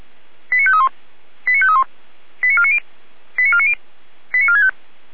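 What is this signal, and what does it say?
Five-tone selective call (ZVEI), heard as quick runs of five tones, tones of about 70 ms. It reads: 89531 89531 89490 89490 89465
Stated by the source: Five-tone ZVEI selective call received by SDR.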